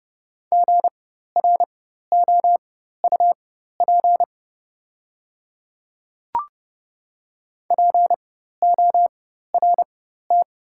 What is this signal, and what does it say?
Morse code, heard as a keyed tone, about 30 wpm, spaced as for 14 wpm, 700 Hz. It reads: GROUP PORT
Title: 4x Character Spacing - Top 300 Words - 30wpm